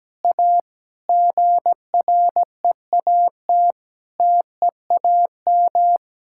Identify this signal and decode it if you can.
Morse code, heard as a keyed tone, about 17 wpm, 700 Hz. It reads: A GREAT TEAM
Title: Sentences from Top 1600 Words 17wpm